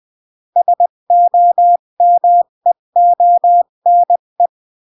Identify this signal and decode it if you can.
Morse code, heard as a keyed tone, about 20 wpm, 700 Hz. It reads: SOMEONE
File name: CW / jodorowsky / 82 commandments 20-15